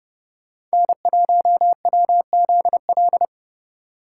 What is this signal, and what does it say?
Morse code, heard as a keyed tone, about 34 wpm, 700 Hz. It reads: N1WZL